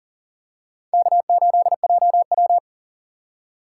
Morse code, heard as a keyed tone, about 40 wpm, 700 Hz. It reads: K8JW
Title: USA Call Signs - 4 Characters - 2x Character Spacing - 40wpm